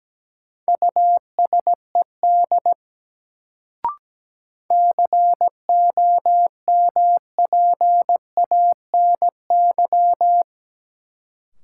Morse code, heard as a keyed tone, about 17 wpm, 700 Hz. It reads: USED COMPANY
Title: Sentences from Top 600 Words - New Word Drill - 17wpm